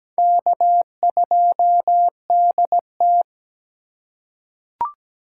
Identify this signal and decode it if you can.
Morse code, heard as a keyed tone, about 17 wpm, 700 Hz. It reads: K2DT
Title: USA Call Signs - 3x Character Spacing - 17wpm